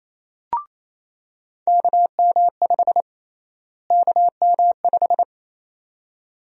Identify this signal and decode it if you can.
Morse code, heard as a keyed tone, about 28 wpm, 700 Hz. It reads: KM5 KM5
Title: USA Call Sign Prefixes - Repeated 2x - 28wpm